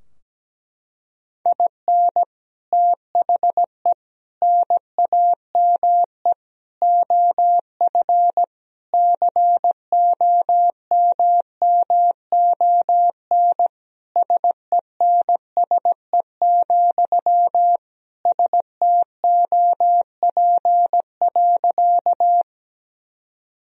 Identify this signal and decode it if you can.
Morse code, heard as a keyed tone, about 17 wpm, 700 Hz. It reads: IN THE NAME OF COMMON SENSE, STOP.
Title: Sentences from Top 1000 Words - Review Part 4 - 17wpm